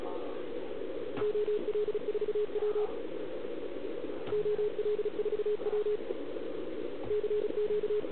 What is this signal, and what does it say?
Morse code, heard as a keyed tone, about 36 wpm, 410 Hz, with radio noise. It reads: OL4W OL4W CQ